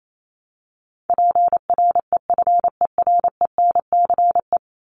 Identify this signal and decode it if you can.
Morse code, heard as a keyed tone, about 28 wpm, 700 Hz. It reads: PREFERENCE
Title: Top 1251-1500 Words 28wpm